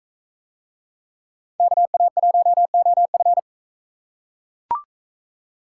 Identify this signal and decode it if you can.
Morse code, heard as a keyed tone, about 40 wpm, 700 Hz. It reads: KA1OF